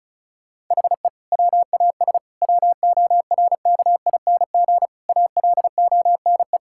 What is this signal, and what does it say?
Morse code, heard as a keyed tone, about 34 wpm, 700 Hz. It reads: HE WAS WORKING ALONE
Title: Sentences from Top 1000 Words - Review Part 5 - 35wpm